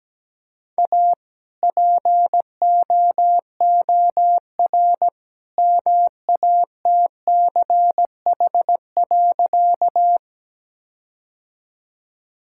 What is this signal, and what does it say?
Morse code, heard as a keyed tone, about 17 wpm, 700 Hz. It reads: A POOR MATCH.
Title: Sentences from Top 1600 Words 17wpm